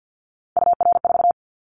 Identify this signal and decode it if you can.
Morse code, heard as a keyed tone, about 50 wpm, 700 Hz. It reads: UR4
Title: Callsign Prefixes for Top 25 DXCC - 50wpm